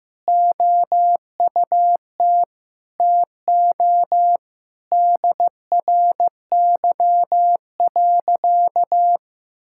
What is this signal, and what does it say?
Morse code, heard as a keyed tone, about 15 wpm, 700 Hz. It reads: OUT TO DRY.
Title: Sentences from Top 1600 Words 15wpm